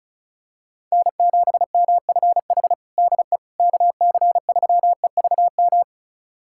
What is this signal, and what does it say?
Morse code, heard as a keyed tone, about 34 wpm, 700 Hz. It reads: N7MFH DE KC3EVM
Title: USA Callsign DE Callsign 35wpm